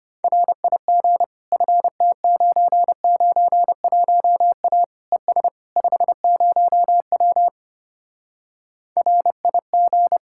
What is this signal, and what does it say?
Morse code, heard as a keyed tone, about 33 wpm, 700 Hz. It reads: RIG FT991A ES 50W RIG